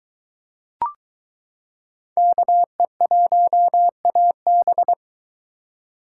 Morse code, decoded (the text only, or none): KE1AB